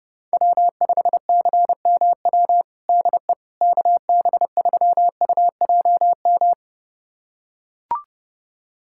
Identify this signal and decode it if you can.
Morse code, heard as a keyed tone, about 30 wpm, 700 Hz. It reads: W5CMW DE KB3UJM
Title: USA Callsign DE Callsign 30wpm